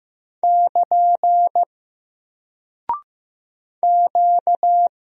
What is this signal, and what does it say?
Morse code, heard as a keyed tone, about 15 wpm, 700 Hz. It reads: <KN> Q
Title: Single Character (Letter:Number:Punctuation:Prosign) Alt 15wpm